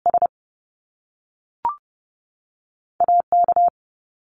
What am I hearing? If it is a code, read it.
Morse code, heard as a keyed tone, about 30 wpm, 700 Hz. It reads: S A K
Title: Top Two Letter Combinations in Words 30wpm